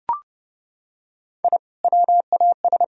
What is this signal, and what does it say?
Morse code, heard as a keyed tone, about 32 wpm, 700 Hz. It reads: I WAS
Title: Sentences from Top 1000 Words - Review Part 1 - 30wpm